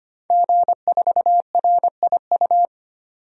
Morse code, heard as a keyed tone, about 25 wpm, 700 Hz. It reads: G4RIU
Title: DXCC-100 Call Signs - 5 Characters - 25wpm